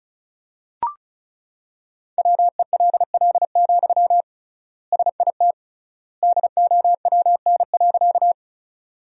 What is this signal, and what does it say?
Morse code, heard as a keyed tone, about 36 wpm, 700 Hz. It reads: WELL, SIT DOWN.
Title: 3x Word-Spacing - Sentences from Top 500 Words - 35wpm